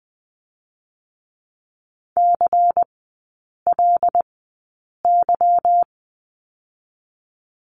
Morse code, heard as a keyed tone, about 20 wpm, spaced as for 8 wpm, 700 Hz. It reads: CLY
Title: 152 Lesson 25e - Letter p - Three Characters 20wpm